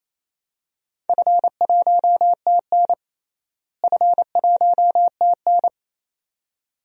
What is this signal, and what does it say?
Morse code, heard as a keyed tone, about 28 wpm, 700 Hz. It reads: F1TN F1TN
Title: DXCC-100 Call Signs - 4 Characters - Repeated 2x - 28wpm